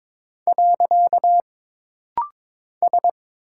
Morse code, heard as a keyed tone, about 22 wpm, 700 Hz. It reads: . S
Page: This rapid-fire timing is geared to push your ICR (Instant Character Recognition) skills.